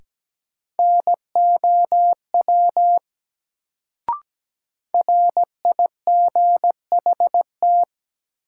Morse code, heard as a keyed tone, about 17 wpm, 700 Hz. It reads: NOW RIGHT
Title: Sets of 2 Words - New Word Drill - 17wpm